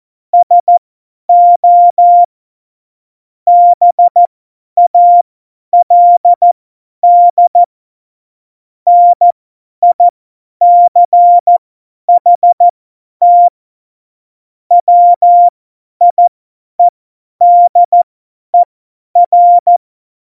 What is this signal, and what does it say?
Morse code, heard as a keyed tone, about 14 wpm, spaced as for 10 wpm, 700 Hz. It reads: SO BALD NICHT WIEDER
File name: Morsekurs / Witze / Witz 29. Dezember 2020, 50 F70